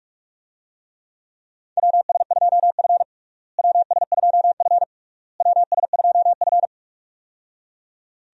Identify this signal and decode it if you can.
Morse code, heard as a keyed tone, about 46 wpm, 700 Hz. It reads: WS2F WS2F WS2F
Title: USA Call Signs - 4 Characters - Repeated 3x - 45wpm